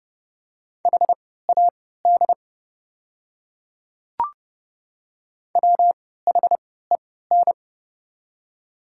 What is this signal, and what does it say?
Morse code, heard as a keyed tone, about 30 wpm, spaced as for 17 wpm, 690 Hz. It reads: HAD WHEN
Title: 3x Character Spacing - Top 100 Words - 30wpm